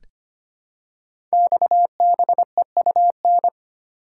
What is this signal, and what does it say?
Morse code, heard as a keyed tone, about 25 wpm, 700 Hz. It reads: XBEUN
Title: Code Groups - 5 Letters - 25wpm